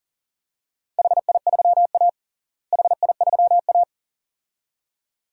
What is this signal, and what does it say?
Morse code, heard as a keyed tone, about 40 wpm, 700 Hz. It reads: HI3A HI3A
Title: DXCC-100 Call Signs - 4 Characters - Repeated 2x - 40wpm